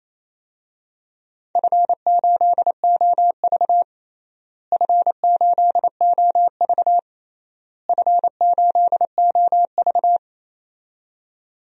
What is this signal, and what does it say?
Morse code, heard as a keyed tone, about 28 wpm, 700 Hz. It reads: F8OV F8OV F8OV
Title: DXCC-100 Call Signs - 4 Characters - Repeated 3x - 28wpm